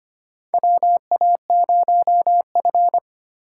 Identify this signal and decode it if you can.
Morse code, heard as a keyed tone, about 25 wpm, 700 Hz. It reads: WA0F
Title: USA Call Signs - 4 Characters - 3x Character Spacing - 25wpm